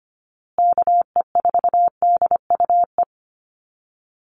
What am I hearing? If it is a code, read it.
Morse code, heard as a keyed tone, about 25 wpm, 700 Hz. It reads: KE4DUE
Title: USA Call Signs Repeated 2x 25wpm